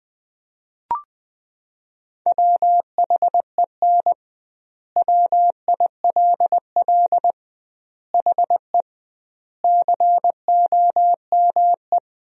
Morse code, heard as a keyed tone, about 20 wpm, 700 Hz. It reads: WHEN WILL HE COME
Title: 2x Word-Spacing - Sentences from Top 200 Words - 20wpm